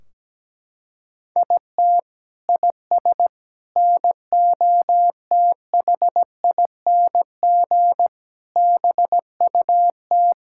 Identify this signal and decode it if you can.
Morse code, heard as a keyed tone, about 17 wpm, 700 Hz. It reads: IT IS NOTHING BUT
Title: Sentences from Top 1600 Words 17wpm